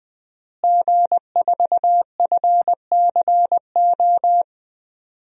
Morse code, heard as a keyed tone, about 20 wpm, 700 Hz. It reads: G4FCO
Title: DXCC-100 Call Signs - 5 Characters - Repeated 3x - 20wpm